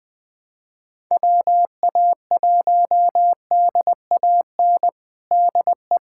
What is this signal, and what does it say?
Morse code, heard as a keyed tone, about 20 wpm, 700 Hz. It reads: WA1DAN DE
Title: USA Callsign DE Callsign 20wpm